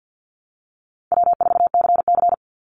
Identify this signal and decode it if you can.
Morse code, heard as a keyed tone, about 50 wpm, 700 Hz. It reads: W4CC
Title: Call Signs - 50wpm